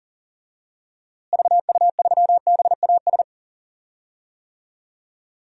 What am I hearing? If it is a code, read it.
Morse code, heard as a keyed tone, about 38 wpm, 700 Hz. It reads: VU3BAS
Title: DXCC-100 Call Signs - 6 Characters - Repeated 2x - 40wpm